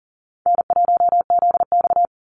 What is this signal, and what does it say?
Morse code, heard as a keyed tone, about 40 wpm, 700 Hz. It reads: N1ZX
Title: Easy Call Signs Repeated 2x - 40wpm